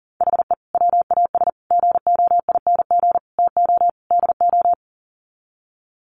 Morse code, heard as a keyed tone, about 40 wpm, 700 Hz. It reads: HE WAS GOING TO DO
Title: Sets of 5 Words Encore2 40wpm